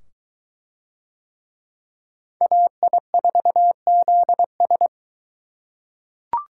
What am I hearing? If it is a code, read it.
Morse code, heard as a keyed tone, about 22 wpm, 700 Hz. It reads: AI4ZS